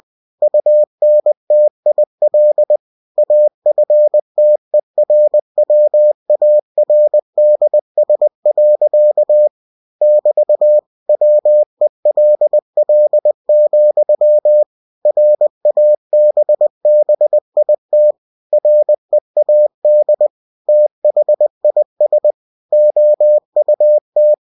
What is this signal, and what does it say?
Morse code, heard as a keyed tone, about 20 wpm, 600 Hz. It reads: UNTIL AFTERWARDS. <BT> WELL, RABBIT READ THIS OUT
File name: cw-books / winnie-the-pooh / winnie-20wpm-page-0064